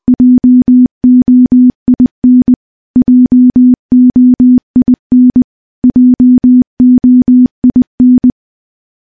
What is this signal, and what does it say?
Morse code, heard as a keyed tone, about 20 wpm, 250 Hz. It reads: JOIN JOIN JOIN
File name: morse / morse-training-20WPM